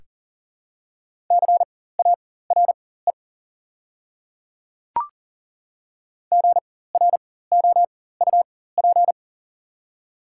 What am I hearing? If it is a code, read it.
Morse code, heard as a keyed tone, about 40 wpm, spaced as for 19 wpm, 690 Hz. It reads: CARE GROUP
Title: 4x Character Spacing - Top 300 Words - 40wpm